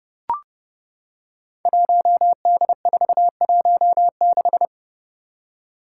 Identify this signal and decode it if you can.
Morse code, heard as a keyed tone, about 30 wpm, 700 Hz. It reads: 1D416
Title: Five Challenging Letter-Number Combinations 30wpm